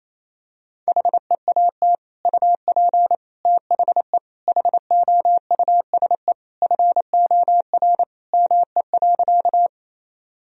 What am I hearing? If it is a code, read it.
Morse code, heard as a keyed tone, about 28 wpm, 700 Hz. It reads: HEAT UP THE HOUSE FOR ME.
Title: Sentences from Top 1000 Words - Review Part 3 - 28wpm